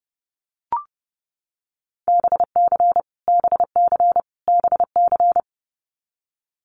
Morse code, heard as a keyed tone, about 30 wpm, 700 Hz. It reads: BC BC BC
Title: Canadian Provinces and Territories - Repeated 3x - 30wpm